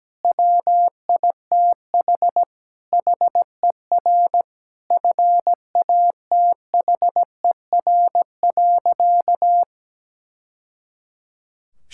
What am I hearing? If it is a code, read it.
Morse code, heard as a keyed tone, about 17 wpm, 700 Hz. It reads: WITH HER FATHER.
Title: Sentences from Top 1000 Words - Review Part 5 - 17wpm